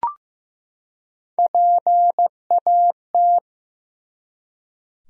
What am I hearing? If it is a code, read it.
Morse code, heard as a keyed tone, about 16 wpm, 700 Hz. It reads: PAT
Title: USA Common Names - Max 5 Letters - 15wpm